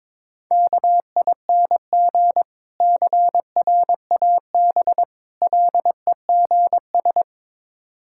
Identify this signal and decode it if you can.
Morse code, heard as a keyed tone, about 22 wpm, 700 Hz. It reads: KING CRAB LEGS
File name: Difficult Words with Hints - 22wpm